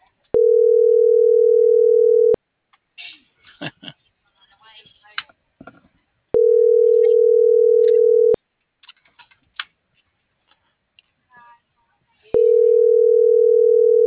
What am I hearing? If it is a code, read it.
Ringback tone (North America), 440 + 480 Hz.
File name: CENTRAL MAYAB / out raulcristinomayab